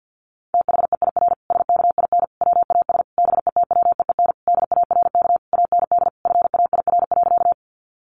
Morse code, heard as a keyed tone, about 50 wpm, 700 Hz. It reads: THEIR SKIN WAS BETWEEN DARK AND FAIR.
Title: Sentences from Top 1300 Words 50wpm